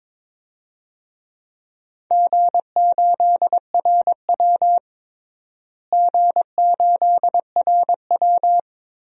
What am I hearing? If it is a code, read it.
Morse code, heard as a keyed tone, about 22 wpm, 700 Hz. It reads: G8RW G8RW
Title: DXCC-100 Call Signs - 4 Characters - Repeated 3x - 22wpm